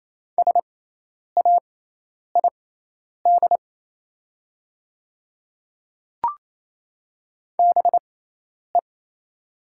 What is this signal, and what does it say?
Morse code, heard as a keyed tone, about 27 wpm, spaced as for 10 wpm, 700 Hz. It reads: SAID BE
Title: 6x Character Spacing - Top 100 Words - 28wpm